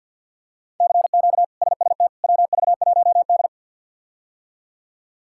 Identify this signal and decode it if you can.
Morse code, heard as a keyed tone, about 47 wpm, 690 Hz. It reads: CQ SST WV1D
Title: K1USN SST QSO Practice - 50wpm